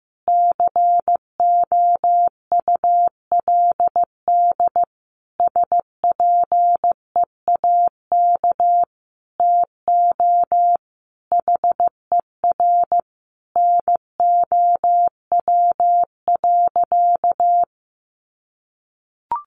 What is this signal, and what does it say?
Morse code, heard as a keyed tone, about 15 wpm, 700 Hz. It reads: COULD SPEAK TO HER NOW.
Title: Sentences from Top 1400 Words 15wpm